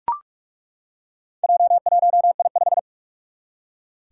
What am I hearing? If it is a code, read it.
Morse code, heard as a keyed tone, about 42 wpm, 700 Hz. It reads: J1I5
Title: Four Challenging Letter-Number Combinations 45wpm